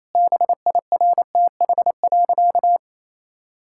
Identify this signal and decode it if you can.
Morse code, heard as a keyed tone, about 28 wpm, 700 Hz. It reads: BIRTH.
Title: Sentences from Top 1000 Words - Review Part 6 - 28wpm